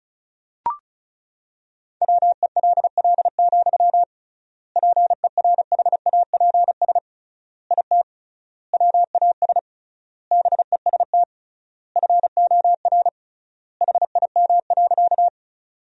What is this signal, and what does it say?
Morse code, heard as a keyed tone, about 36 wpm, 700 Hz. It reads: WELL, PERHAPS IT WAS BEST FOR HIM.
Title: 3x Word-Spacing - Sentences from Top 500 Words - 35wpm